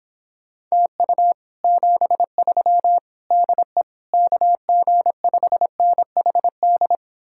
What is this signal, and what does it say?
Morse code, heard as a keyed tone, about 26 wpm, 700 Hz. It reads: TU 73 DE KG5NHD